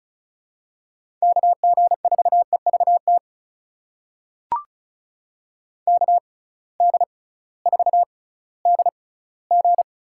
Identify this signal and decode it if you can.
Morse code, heard as a keyed tone, about 36 wpm, spaced as for 19 wpm, 700 Hz. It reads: KG4EVT KD4DG
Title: USA Call Signs - 6x Character Spacing - 35wpm